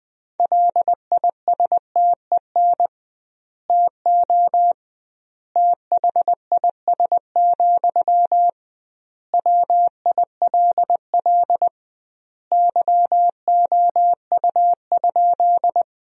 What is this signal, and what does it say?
Morse code, heard as a keyed tone, about 20 wpm, 700 Hz. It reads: LISTEN TO THIS, WILL YOU?
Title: 2x Word-Spacing - Sentences from Top 400 Words - 20wpm